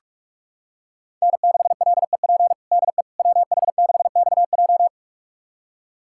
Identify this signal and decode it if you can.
Morse code, heard as a keyed tone, about 45 wpm, 700 Hz. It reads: N6LEP DE WH6XJ
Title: USA Callsign DE Callsign - 5 Characters - 45wpm